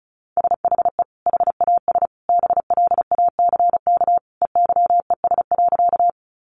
Morse code, heard as a keyed tone, about 34 wpm, 700 Hz. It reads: SHE HAS BLACK EYES.
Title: Easy Phrases 35wpm